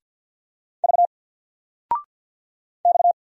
Morse code, heard as a keyed tone, about 50 wpm, 700 Hz. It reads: V X
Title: Single Letter-Number Alt 50wpm